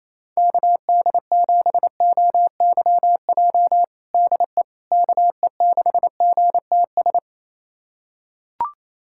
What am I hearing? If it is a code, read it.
Morse code, heard as a keyed tone, about 28 wpm, 700 Hz. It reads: KD7OYJ DE KE6GTS
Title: USA Callsign DE Callsign - 6 Characters - 28wpm